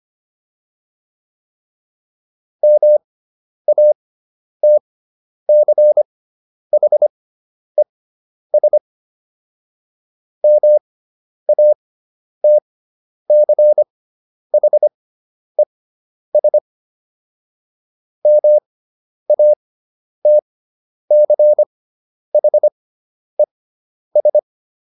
Morse code, heard as a keyed tone, about 25 wpm, spaced as for 10 wpm, 600 Hz. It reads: MATCHES MATCHES MATCHES